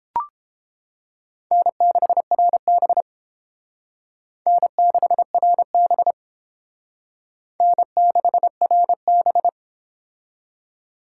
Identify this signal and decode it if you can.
Morse code, heard as a keyed tone, about 31 wpm, 700 Hz. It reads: N6RB N6RB N6RB